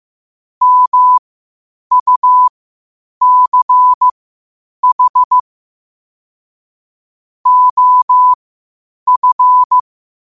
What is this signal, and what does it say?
Morse code, heard as a keyed tone, about 15 wpm, spaced as for 8 wpm, 1000 Hz. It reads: MUCH OF
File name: qotdfiles / 8wpm